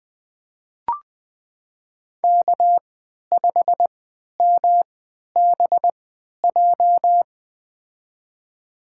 Morse code, heard as a keyed tone, about 20 wpm, spaced as for 11 wpm, 700 Hz. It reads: K5MBJ
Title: USA Call Signs - 5 Characters - 3x Character Spacing - 20wpm